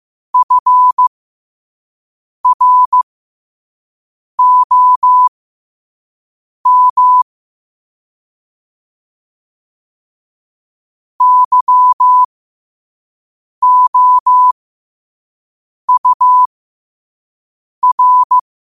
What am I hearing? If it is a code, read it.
Morse code, heard as a keyed tone, about 15 wpm, spaced as for 5 wpm, 1000 Hz. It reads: FROM YOUR
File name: 5wpm